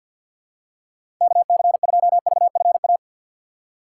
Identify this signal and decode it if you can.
Morse code, heard as a keyed tone, about 52 wpm, 700 Hz. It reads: KC2VFA
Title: USA Call Signs - 6 Characters - 3x Character Spacing - 50wpm